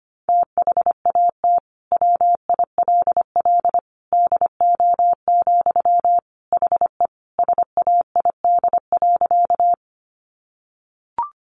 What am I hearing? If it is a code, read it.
Morse code, heard as a keyed tone, about 25 wpm, 700 Hz. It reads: THAT WILL DO, HE SAID.
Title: Sentences from Top 100 Words 25wpm